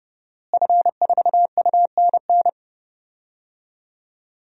Morse code, heard as a keyed tone, about 30 wpm, 700 Hz. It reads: F4UNN
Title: DXCC-100 Call Signs - 5 Characters - Repeated 2x - 30wpm